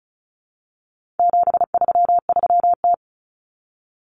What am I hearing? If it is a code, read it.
Morse code, heard as a keyed tone, about 34 wpm, 700 Hz. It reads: 733T
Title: Common Cut Numbers - 4-Digit 35wpm